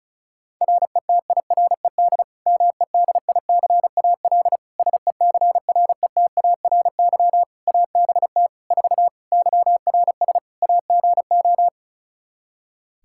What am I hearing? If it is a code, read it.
Morse code, heard as a keyed tone, about 34 wpm, 700 Hz. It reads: RETIRED MEDICAL SECRETARY ABT 4 YRS AGO
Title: Practice QSO - Exchange 3 - More Info - 35wpm